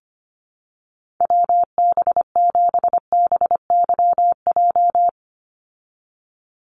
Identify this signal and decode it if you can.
Morse code, heard as a keyed tone, about 25 wpm, 700 Hz. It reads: WB7BYJ
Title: USA Call Signs 25wpm